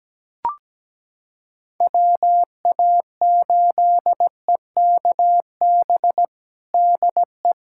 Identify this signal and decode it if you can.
Morse code, heard as a keyed tone, about 17 wpm, 700 Hz. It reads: WA8EKB DE
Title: USA Callsign DE Callsign - 6 Characters - 17wpm